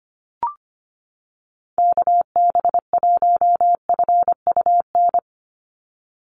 Morse code, heard as a keyed tone, about 25 wpm, 700 Hz. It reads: KB1FUN
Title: USA Call Signs 25wpm